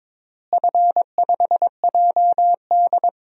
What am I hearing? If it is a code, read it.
Morse code, heard as a keyed tone, about 22 wpm, 700 Hz. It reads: F5JD
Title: DXCC-100 Call Signs - 4 Characters - Repeated 3x - 22wpm